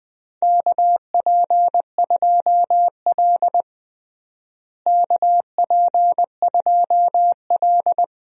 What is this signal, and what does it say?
Morse code, heard as a keyed tone, about 20 wpm, 700 Hz. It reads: KP2L KP2L
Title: DXCC-100 Call Signs - 4 Characters - Repeated 2x - 20wpm